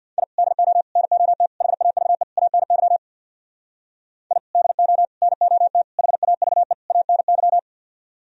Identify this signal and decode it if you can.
Morse code, heard as a keyed tone, about 63 wpm, 690 Hz. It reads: I DO NOT HAVE ANY I DO NOT HAVE ANY